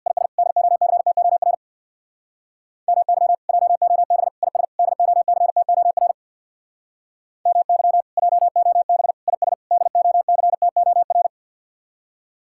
Morse code, heard as a keyed tone, about 63 wpm, 700 Hz. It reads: IS DOCTOR MY JOB IS DOCTOR MY JOB IS DOCTOR